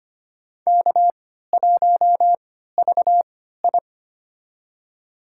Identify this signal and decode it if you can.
Morse code, heard as a keyed tone, about 25 wpm, spaced as for 14 wpm, 700 Hz. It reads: K1VI
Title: USA Call Signs - 3x Character Spacing - 25wpm